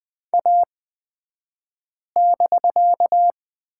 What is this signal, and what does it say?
Morse code, heard as a keyed tone, about 20 wpm, spaced as for 5 wpm, 700 Hz. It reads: A<BK>
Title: 249 Lesson 40d - BK - Two Characters 20wpm